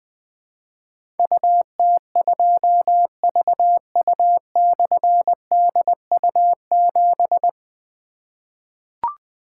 Morse code, heard as a keyed tone, about 20 wpm, 700 Hz. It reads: UT2VU/DU7
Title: Difficult Call Signs 20wpm